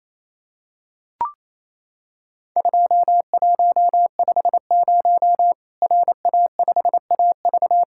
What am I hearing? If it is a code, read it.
Morse code, heard as a keyed tone, about 28 wpm, 700 Hz. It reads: 2150 RA5AV
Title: Worked All Europe DX Contest - QTC Exchange - 28wpm